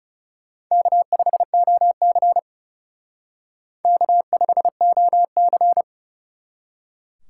Morse code, heard as a keyed tone, about 36 wpm, 700 Hz. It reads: K5OC K5OC
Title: Speed Racing - USA 1x2 Call Signs 30wpm